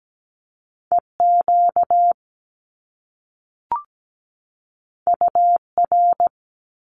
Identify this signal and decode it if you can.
Morse code, heard as a keyed tone, about 17 wpm, 700 Hz. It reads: EQ UR
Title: Top Two Letter Combinations in Words 17wpm